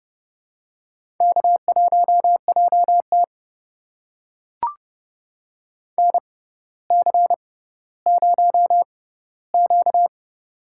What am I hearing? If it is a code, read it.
Morse code, heard as a keyed tone, about 31 wpm, spaced as for 16 wpm, 700 Hz. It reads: K1JT NC0Q
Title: USA Call Signs - 4 Characters - 6x Character Spacing - 30wpm